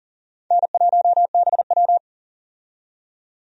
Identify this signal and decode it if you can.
Morse code, heard as a keyed tone, about 40 wpm, 700 Hz. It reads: N1BW
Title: USA Call Signs - 4 Characters - 40wpm